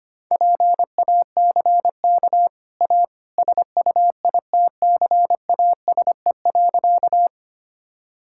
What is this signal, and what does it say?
Morse code, heard as a keyed tone, about 25 wpm, 700 Hz. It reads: PACK A SUITCASE.
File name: Easy Sentences with Hints - 25wpm